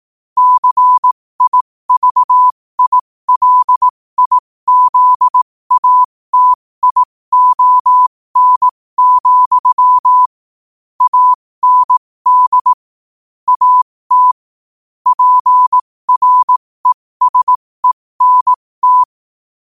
Morse code, heard as a keyed tone, about 18 wpm, spaced as for 15 wpm, 1000 Hz. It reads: CIVILIZATION, AND AT PRESENT
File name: qotdfiles / 15wpm